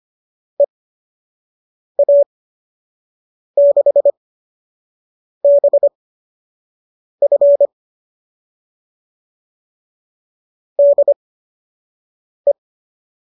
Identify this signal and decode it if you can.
Morse code, heard as a keyed tone, about 25 wpm, spaced as for 6 wpm, 570 Hz. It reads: EA6BF DE